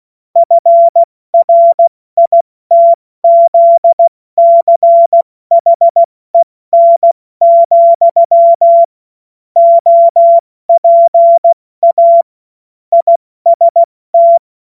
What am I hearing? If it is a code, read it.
Morse code, heard as a keyed tone, about 16 wpm, 680 Hz. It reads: FRITZCHEN, OPA IST
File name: Morsekurs / Witze / Witz 28. September 2021, 70 F80